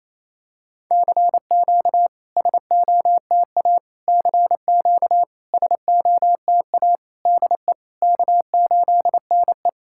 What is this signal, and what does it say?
Morse code, heard as a keyed tone, about 27 wpm, 690 Hz. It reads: CQ SOTA CQ SOTA DE K8NE
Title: SOTA QSO Practice 28wpm